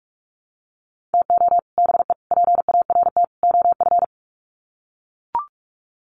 Morse code, heard as a keyed tone, about 44 wpm, 700 Hz. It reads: TO BE PART OF
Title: Sets of 4 Words 45wpm